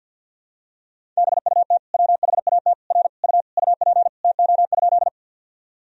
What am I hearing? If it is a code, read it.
Morse code, heard as a keyed tone, about 51 wpm, 700 Hz. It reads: BUT WHAT R U UP TO?
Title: Sentences from Top 100 Words and Abbreviations 50wpm